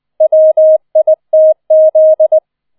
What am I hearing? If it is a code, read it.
Morse code, heard as a keyed tone, about 20 wpm, 630 Hz. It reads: WITZ
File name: 20wpm